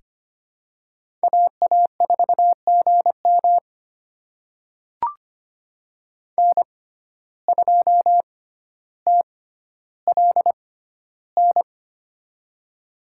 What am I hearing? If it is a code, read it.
Morse code, heard as a keyed tone, about 25 wpm, spaced as for 13 wpm, 700 Hz. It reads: AA4GM N2TLN
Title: USA Call Signs - 5 Characters - 6x Character Spacing - 25wpm